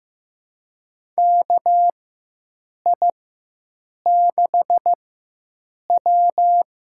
Morse code, heard as a keyed tone, about 16 wpm, spaced as for 7 wpm, 700 Hz. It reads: KI6W
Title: USA Call Signs - 4x Character Spacing - 15wpm